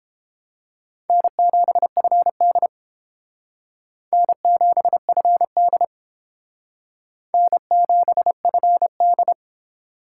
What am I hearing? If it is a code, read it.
Morse code, heard as a keyed tone, about 30 wpm, 700 Hz. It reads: N7FD N7FD N7FD